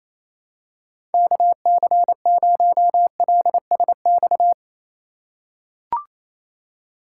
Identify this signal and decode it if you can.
Morse code, heard as a keyed tone, about 28 wpm, 700 Hz. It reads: KC0LSX
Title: USA Call Signs - 3x Character Spacing - 28wpm